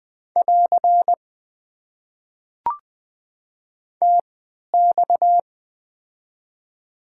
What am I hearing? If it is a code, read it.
Morse code, heard as a keyed tone, about 21 wpm, spaced as for 12 wpm, 700 Hz. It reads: <AR> TX
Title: Top 90 QSO Elements - 3x Character Spacing - 20wpm